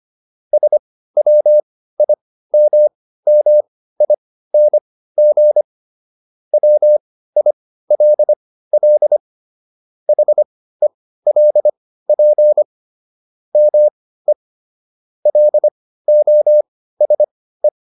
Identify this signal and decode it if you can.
Morse code, heard as a keyed tone, about 25 wpm, spaced as for 15 wpm, 600 Hz. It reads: SWIMMING WILL HELP ME LOSE